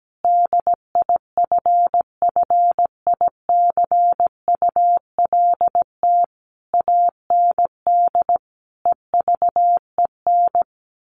Morse code, heard as a keyed tone, about 17 wpm, 700 Hz. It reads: DIFFICULT AND EVEN
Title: Sentences from Top 1200 Words 17wpm